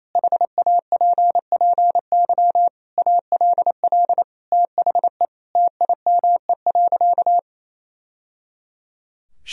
Morse code, heard as a keyed tone, about 28 wpm, 700 Hz. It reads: HAPPY ALL THE TIME.
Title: Sentences from Top 1000 Words - Review Part 4 - 28wpm